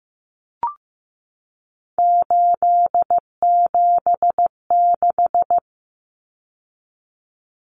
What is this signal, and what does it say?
Morse code, heard as a keyed tone, about 15 wpm, 700 Hz. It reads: 876
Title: Numbers - 3-Digit 15wpm